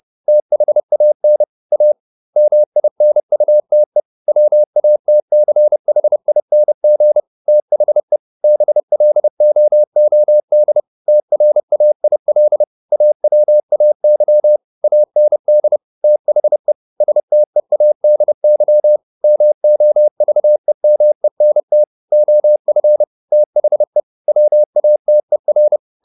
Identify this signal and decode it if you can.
Morse code, heard as a keyed tone, about 30 wpm, 600 Hz. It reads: THAN A MINUTE WATCHING THE BLOOD TRAIL AWAY AND THE STEADY MOVEMENT OF THE WATER
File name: the old man and the sea / the old man and the sea 1 30wpm